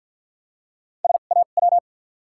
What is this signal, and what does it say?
Morse code, heard as a keyed tone, about 47 wpm, spaced as for 35 wpm, 700 Hz. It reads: SAW
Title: 2x Character Spacing - Top 300 Words - 50wpm